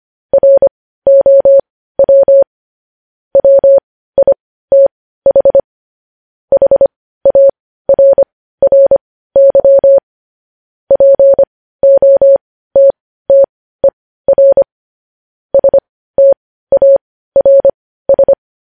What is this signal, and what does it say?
Morse code, heard as a keyed tone, about 25 wpm, spaced as for 15 wpm, 550 Hz. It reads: ROW WITH HARRY POTTER STARS